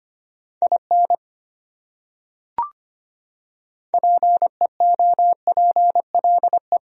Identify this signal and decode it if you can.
Morse code, heard as a keyed tone, about 25 wpm, 700 Hz. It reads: IN PEOPLE
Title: Top 100 Words Repeated 25wpm